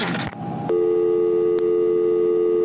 Dial tone (North America), 350 + 440 Hz.
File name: phone numbers / calls / michelle 03 var1